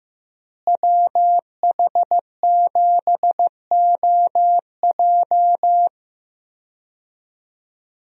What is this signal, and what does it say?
Morse code, heard as a keyed tone, about 15 wpm, 700 Hz. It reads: WH7OJ
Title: USA Call Signs - 5 Characters - Repeated 3x - 15wpm